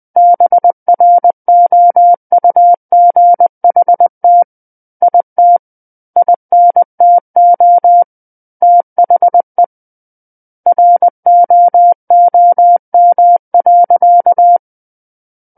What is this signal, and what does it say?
Morse code, heard as a keyed tone, about 20 wpm, 700 Hz. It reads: BROUGHT IT INTO THE ROOM.
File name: 20 WPM